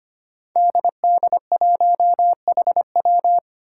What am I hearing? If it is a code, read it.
Morse code, heard as a keyed tone, about 25 wpm, 700 Hz. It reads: DD1HW
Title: Five Challenging Letter-Number Combinations 25wpm